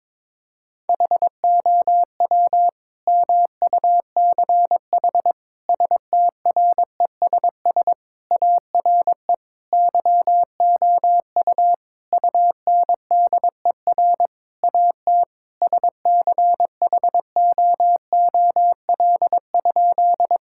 Morse code, heard as a keyed tone, about 22 wpm, 700 Hz. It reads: HOW MUCH STRESS ARE YOU UNDER AT SCHOOL?
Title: Sentences from Top 1000 Words - Review Part 6 - 22wpm